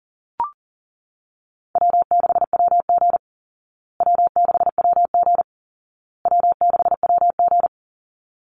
Morse code, heard as a keyed tone, about 40 wpm, 700 Hz. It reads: W6WG W6WG W6WG
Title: Easy Call Signs Repeated 3x - 40wpm